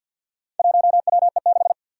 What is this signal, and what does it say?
Morse code, heard as a keyed tone, about 54 wpm, 700 Hz. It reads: 1WE6
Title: Four Challenging Letter-Number Combinations 50wpm